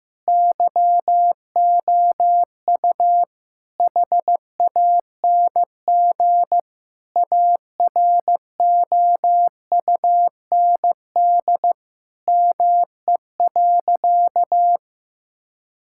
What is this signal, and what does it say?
Morse code, heard as a keyed tone, about 16 wpm, 700 Hz. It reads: YOU HANG AROUND ME.
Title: Sentences from Top 1600 Words 15wpm